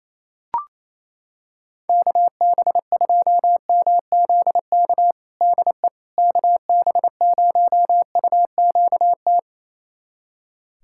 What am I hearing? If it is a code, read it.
Morse code, heard as a keyed tone, about 28 wpm, 700 Hz. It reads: KB2MZK DE KB0UQT
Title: USA Callsign DE Callsign 28wpm